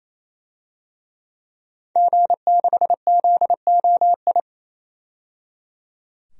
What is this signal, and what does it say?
Morse code, heard as a keyed tone, about 28 wpm, 700 Hz. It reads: G6ZOI
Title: DXCC-100 Call Signs - 5 Characters - 28wpm